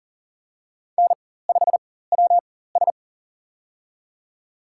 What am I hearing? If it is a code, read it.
Morse code, heard as a keyed tone, about 40 wpm, spaced as for 19 wpm, 700 Hz. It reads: N5WS
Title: USA Call Signs - 4 Characters - 4x Character Spacing - 40wpm